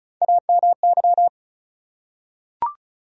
Morse code, heard as a keyed tone, about 35 wpm, 700 Hz. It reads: AMY
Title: USA Common Names - Max 4 Letters - 35wpm